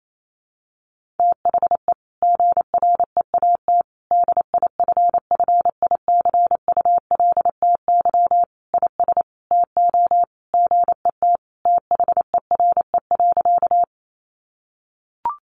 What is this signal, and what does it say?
Morse code, heard as a keyed tone, about 28 wpm, 700 Hz. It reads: THE GREAT DIFFICULTY IS TO GET THERE.
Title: Sentences from Top 1500 Words 28wpm